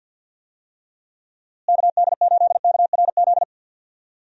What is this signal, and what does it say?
Morse code, heard as a keyed tone, about 48 wpm, 700 Hz. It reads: KD8KRZ
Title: USA Call Signs - 6 Characters - Repeated 2x - 50wpm